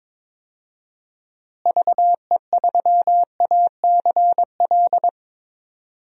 Morse code, heard as a keyed tone, about 22 wpm, 700 Hz. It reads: VE3ACL
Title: DXCC-100 Call Signs - 6 Characters - Repeated 3x - 22wpm